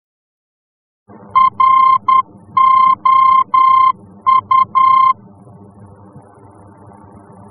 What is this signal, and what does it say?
Morse code, heard as a keyed tone, about 10 wpm, 1050 Hz. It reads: ROU
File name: france / mx / ROU-F-VOR